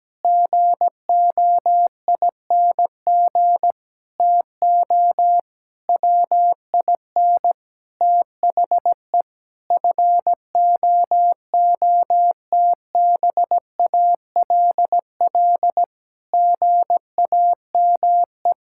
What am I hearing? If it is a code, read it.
Morse code, heard as a keyed tone, about 17 wpm, 700 Hz. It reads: GOING TO WIN THE FOOTBALL GAME
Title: Sentences from Top 1600 Words 17wpm